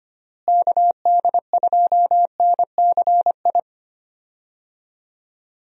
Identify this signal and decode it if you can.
Morse code, heard as a keyed tone, about 25 wpm, 700 Hz. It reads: KD2NCI
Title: USA Call Signs - 6 Characters - Repeated 2x - 25wpm